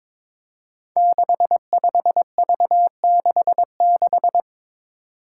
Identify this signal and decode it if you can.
Morse code, heard as a keyed tone, about 22 wpm, 700 Hz. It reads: 65V66
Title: Five Challenging Letter-Number Combinations 22wpm